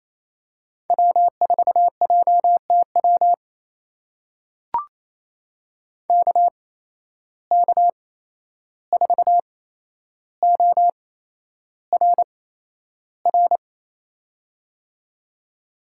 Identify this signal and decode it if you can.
Morse code, heard as a keyed tone, about 28 wpm, spaced as for 11 wpm, 700 Hz. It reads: W4JTW KK4ORR
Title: USA Call Signs - 8x Character Spacing - 28wpm